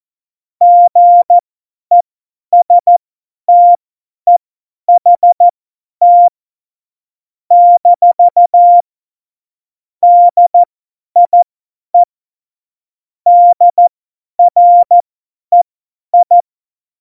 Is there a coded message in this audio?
Yes: Morse code from 0.6 to 16.4 s, heard as a keyed tone, about 14 wpm, spaced as for 10 wpm, 700 Hz. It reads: GESTEHT - DIE DREI